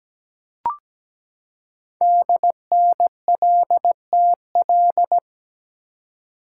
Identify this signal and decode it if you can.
Morse code, heard as a keyed tone, about 17 wpm, 700 Hz. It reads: DNLTL
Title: Code Groups - 5 Letters - 17wpm